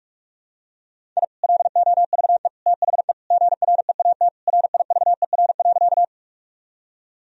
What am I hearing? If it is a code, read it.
Morse code, heard as a keyed tone, about 46 wpm, 700 Hz. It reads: I LOVE THE GREAT RIVER.
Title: Sentences from Top 1000 Words - Review Part 1 - 45wpm